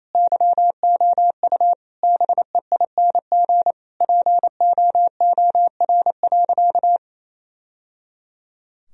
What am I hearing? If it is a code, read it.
Morse code, heard as a keyed tone, about 28 wpm, 700 Hz. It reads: YOU BEING POOR.
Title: Sentences from Top 1000 Words - Review Part 3 - 28wpm